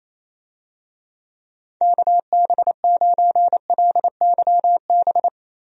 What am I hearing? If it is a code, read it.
Morse code, heard as a keyed tone, about 28 wpm, 700 Hz. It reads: KB9LYB
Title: USA Call Signs - 6 Characters - 28wpm